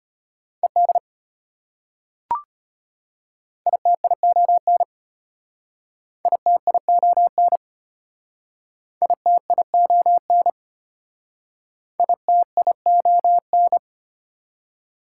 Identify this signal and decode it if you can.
Morse code, heard as a keyed tone, about 33 wpm, 690 Hz. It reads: ED ITION ITION ITION ITION